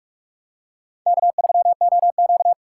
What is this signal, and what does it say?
Morse code, heard as a keyed tone, about 44 wpm, 700 Hz. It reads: K3OQ